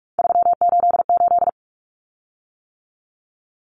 Morse code, heard as a keyed tone, about 44 wpm, 700 Hz. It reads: 388
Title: Common Cut Numbers - 3-Digit 45wpm